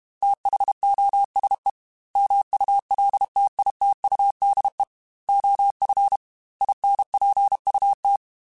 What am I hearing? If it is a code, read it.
Morse code, heard as a keyed tone, about 32 wpm, 800 Hz. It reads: THOSE MULTITUDE OF INPUT
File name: Technical Subjects / Technical Subjects-mp3s / OR Gate @30